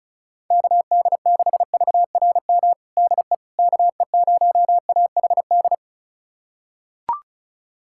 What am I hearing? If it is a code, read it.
Morse code, heard as a keyed tone, about 36 wpm, 700 Hz. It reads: KD6VRM DE KE0AHD
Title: USA Callsign DE Callsign 35wpm